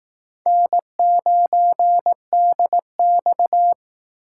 Morse code, heard as a keyed tone, about 18 wpm, 700 Hz. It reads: N9DX
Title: Speed Racing - USA 1x2 Call Signs 15wpm